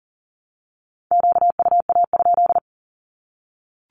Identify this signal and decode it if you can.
Morse code, heard as a keyed tone, about 39 wpm, 700 Hz. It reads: QUA?
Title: Q-Codes Repeated 3x 40wpm